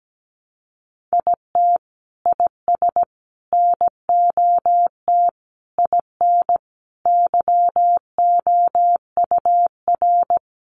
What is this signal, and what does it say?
Morse code, heard as a keyed tone, about 17 wpm, 700 Hz. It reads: IT IS NOT IN YOUR
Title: Sentences from Top 1400 Words 17wpm